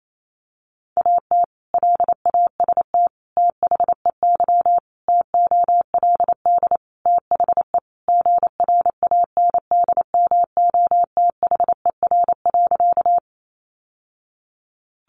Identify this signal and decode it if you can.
Morse code, heard as a keyed tone, about 28 wpm, 700 Hz. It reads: AT LAST THEY TOLD THE GRANDMOTHER.
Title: Sentences from Top 1200 Words 28wpm